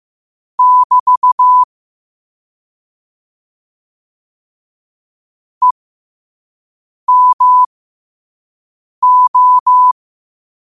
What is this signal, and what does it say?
Morse code, heard as a keyed tone, about 15 wpm, spaced as for 5 wpm, 1000 Hz. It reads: <BT> EMO
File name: qotdfiles / 5wpm